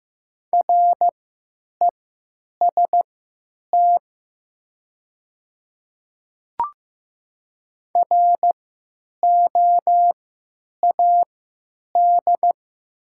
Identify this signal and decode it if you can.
Morse code, heard as a keyed tone, about 15 wpm, spaced as for 9 wpm, 700 Hz. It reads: REST ROAD
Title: 3x Character Spacing - Top 500 Words - 15wpm